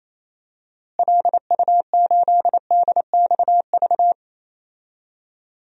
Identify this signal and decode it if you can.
Morse code, heard as a keyed tone, about 28 wpm, 700 Hz. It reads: LU8DXV
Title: DXCC-100 Call Signs - 6 Characters - Repeated 2x - 28wpm